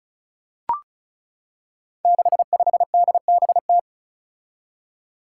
Morse code, heard as a keyed tone, about 36 wpm, 700 Hz. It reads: 65DBT
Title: Five Challenging Letter-Number Combinations 35wpm